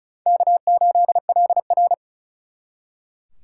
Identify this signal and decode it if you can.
Morse code, heard as a keyed tone, about 47 wpm, spaced as for 41 wpm, 700 Hz. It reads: K8LR
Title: Speed Racing - USA 1x2 Call Signs 35wpm